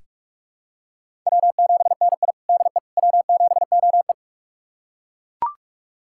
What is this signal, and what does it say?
Morse code, heard as a keyed tone, about 41 wpm, 700 Hz. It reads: W7NI DE W7OE
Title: USA Callsign DE Callsign - 4 Characters - 45wpm